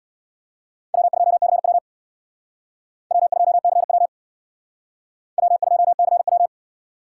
Morse code, heard as a keyed tone, about 64 wpm, 700 Hz. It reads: W2CW W2CW W2CW